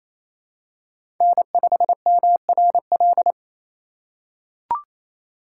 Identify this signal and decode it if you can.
Morse code, heard as a keyed tone, about 28 wpm, 700 Hz. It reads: N5MRL
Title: USA Call Signs - 5 Characters - 2x Character Spacing - 28wpm